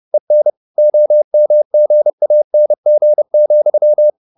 Morse code, heard as a keyed tone, about 30 wpm, 600 Hz. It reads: EN OMGANG,
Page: Ferie 30 wpm